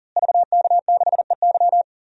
Morse code, heard as a keyed tone, about 39 wpm, 700 Hz. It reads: VK6EY
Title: DXCC-100 Call Signs - 5 Characters - Repeated 2x - 40wpm